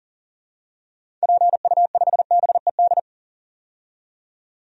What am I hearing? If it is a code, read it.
Morse code, heard as a keyed tone, about 40 wpm, 700 Hz. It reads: PU5BED
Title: DXCC-100 Call Signs - 6 Characters - 40wpm